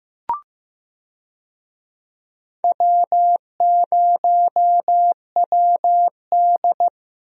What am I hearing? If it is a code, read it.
Morse code, heard as a keyed tone, about 15 wpm, 700 Hz. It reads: W0WD
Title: USA Call Signs - 4 Characters - Repeated 2x - 15wpm